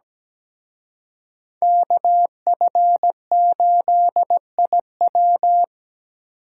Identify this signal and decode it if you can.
Morse code, heard as a keyed tone, about 20 wpm, 700 Hz. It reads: KF8IW
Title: Speed Racing - USA 2x2 Call Signs 17wpm